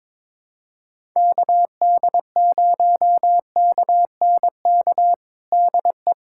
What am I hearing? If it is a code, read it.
Morse code, heard as a keyed tone, about 21 wpm, 700 Hz. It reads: KD0KNK DE
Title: USA Callsign DE Callsign - 6 Characters - 22wpm